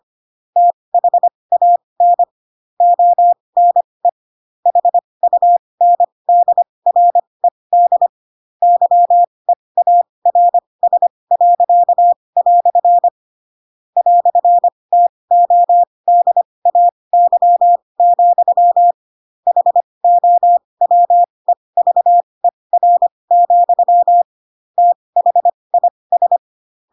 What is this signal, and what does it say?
Morse code, heard as a keyed tone, about 25 wpm, spaced as for 20 wpm, 700 Hz. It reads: THAN ONE HUNDRED YEARS." "TODAY, HOWEVER, THIS
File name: Pinocchio / Pinocchio ch4 15 25 ik1hge